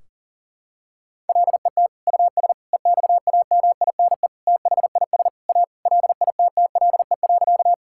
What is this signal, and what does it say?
Morse code, heard as a keyed tone, about 40 wpm, 700 Hz. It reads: LET US EXAMINE THIS A LITTLE.
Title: Sentences from Top 1600 Words 40wpm